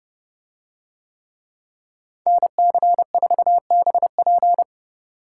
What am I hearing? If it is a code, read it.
Morse code, heard as a keyed tone, about 30 wpm, 700 Hz. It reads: NC4BP